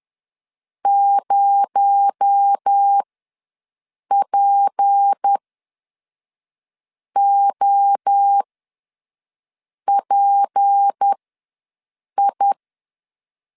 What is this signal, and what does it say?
Morse code, heard as a keyed tone, about 10 wpm, spaced as for 5 wpm, 800 Hz. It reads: 0POPI